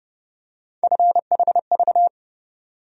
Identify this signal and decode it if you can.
Morse code, heard as a keyed tone, about 30 wpm, 700 Hz. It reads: FHV
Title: USA Call Sign Postfixes - 3 Letters 30wpm